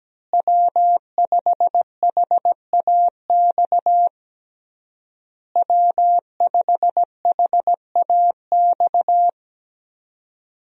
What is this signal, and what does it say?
Morse code, heard as a keyed tone, about 17 wpm, 700 Hz. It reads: W5HAX W5HAX
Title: USA Call Signs - 5 Characters - Repeated 3x - 17wpm